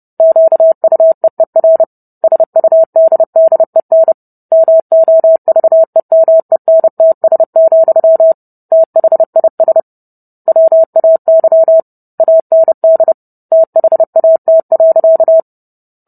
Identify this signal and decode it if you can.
Morse code, heard as a keyed tone, about 30 wpm, 650 Hz. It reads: QUEER SUDDEN MOVEMENTS, THIS WAY AND THAT.